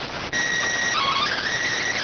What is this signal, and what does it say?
SSTV (slow-scan television), PD 290.